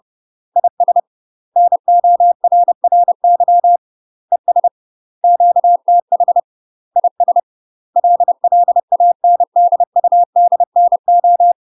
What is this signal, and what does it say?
Morse code, heard as a keyed tone, about 30 wpm, 700 Hz. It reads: IS NORRY ES QTH IS LLANDUDNO
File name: UK MASTER QSOs TRAINING / 30WPM / UK MASTER QSOs 0031